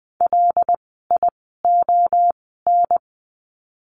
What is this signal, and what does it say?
Morse code, heard as a keyed tone, about 20 wpm, spaced as for 14 wpm, 700 Hz. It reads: LION
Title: 075 Lesson 13f - Letter L - Words 20wpm